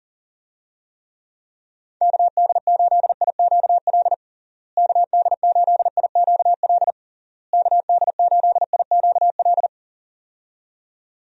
Morse code, heard as a keyed tone, about 40 wpm, 700 Hz. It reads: KD8IQL KD8IQL KD8IQL
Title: USA Call Signs - 6 Characters - Repeated 3x - 40wpm